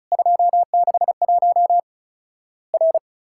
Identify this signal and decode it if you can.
Morse code, heard as a keyed tone, about 36 wpm, 700 Hz. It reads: 261 R
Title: Worked All Europe DX Contest - QTC Exchange - 35wpm